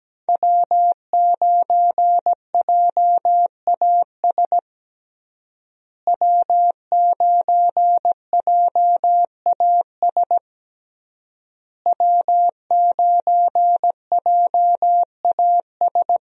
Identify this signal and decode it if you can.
Morse code, heard as a keyed tone, about 17 wpm, 700 Hz. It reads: W9JAS W9JAS W9JAS
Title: USA Call Signs - 5 Characters - Repeated 3x - 17wpm